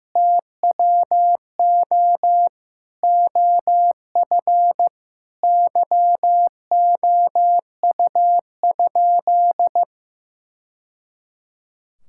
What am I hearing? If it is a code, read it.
Morse code, heard as a keyed tone, about 15 wpm, 700 Hz. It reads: TWO OF YOU?
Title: Sentences from Top 1000 Words - Review Part 6 - 15wpm